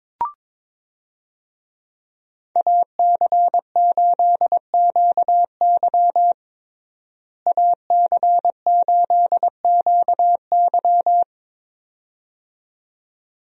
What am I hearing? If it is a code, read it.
Morse code, heard as a keyed tone, about 22 wpm, 700 Hz. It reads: AC8QY AC8QY
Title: USA Call Signs - 5 Characters - Repeated 2x - 22wpm